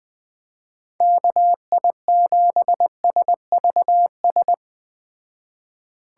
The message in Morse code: KI7SVS

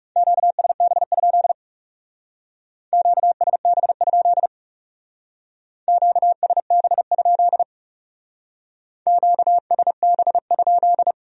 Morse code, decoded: QSB? QSB? QSB? QSB?

